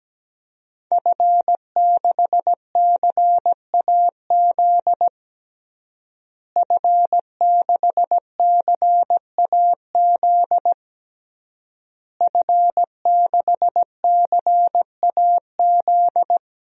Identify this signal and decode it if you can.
Morse code, heard as a keyed tone, about 17 wpm, 700 Hz. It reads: F6CAZ F6CAZ F6CAZ
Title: DXCC-100 Call Signs - 5 Characters - Repeated 3x - 17wpm